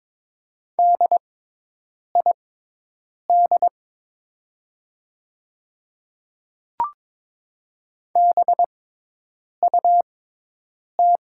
Morse code, heard as a keyed tone, about 23 wpm, spaced as for 8 wpm, 700 Hz. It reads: DID BUT
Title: 6x Character Spacing - Top 100 Words - 22wpm